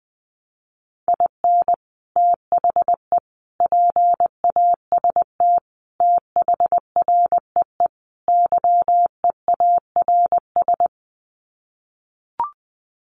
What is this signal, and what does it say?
Morse code, heard as a keyed tone, about 20 wpm, 700 Hz. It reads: IN THE PAST THREE YEARS
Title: Sets of 5 Words Encore2 20wpm